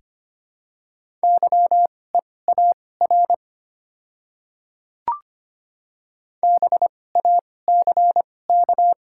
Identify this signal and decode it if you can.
Morse code, heard as a keyed tone, about 25 wpm, spaced as for 18 wpm, 690 Hz. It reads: YEAR BACK
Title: 2x Character Spacing - Top 300 Words - 25wpm